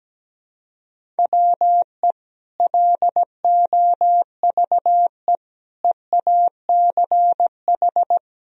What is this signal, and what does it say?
Morse code, heard as a keyed tone, about 17 wpm, 700 Hz. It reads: WE LOVE EACH
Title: Sentences from Top 1000 Words - Review Part 1 - 17wpm